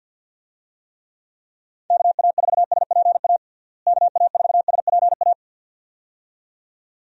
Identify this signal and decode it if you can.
Morse code, heard as a keyed tone, about 51 wpm, 700 Hz. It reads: KA4SPA KA4SPA
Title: USA Call Signs - 6 Characters - Repeated 2x - 50wpm